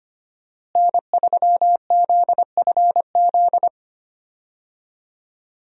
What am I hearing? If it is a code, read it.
Morse code, heard as a keyed tone, about 25 wpm, 700 Hz. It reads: N3ZFZ
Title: USA Call Signs - 5 Characters - Repeated 2x - 25wpm